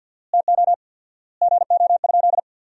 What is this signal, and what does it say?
Morse code, heard as a keyed tone, about 46 wpm, 700 Hz. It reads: TO GO?
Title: 4x Word-Spacing - Sentences from Top 300 Words - 50wpm